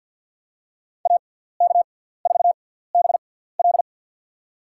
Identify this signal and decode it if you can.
Morse code, heard as a keyed tone, about 50 wpm, spaced as for 17 wpm, 700 Hz. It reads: AK4BL
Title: USA Call Signs - 5 Characters - 6x Character Spacing - 50wpm